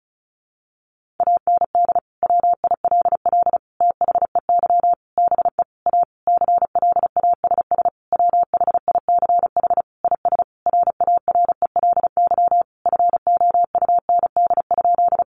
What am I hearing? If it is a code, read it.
Morse code, heard as a keyed tone, about 34 wpm, 700 Hz. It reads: AND WILL THEY BE A CLASS WHICH IS RARELY FOUND?
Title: Sentences from Top 1400 Words 35wpm